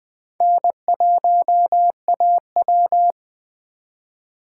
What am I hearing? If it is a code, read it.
Morse code, heard as a keyed tone, about 20 wpm, 700 Hz. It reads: N1AW
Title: 2023 SOTA Chaser Call Signs - 20wpm